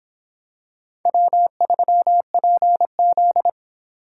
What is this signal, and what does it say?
Morse code, heard as a keyed tone, about 25 wpm, 700 Hz. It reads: W3PZ